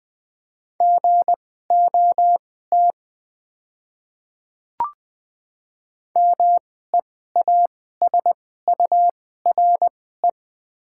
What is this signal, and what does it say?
Morse code, heard as a keyed tone, about 21 wpm, spaced as for 15 wpm, 700 Hz. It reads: GOT MEASURE
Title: 2x Character Spacing - Top 500 Words - 20wpm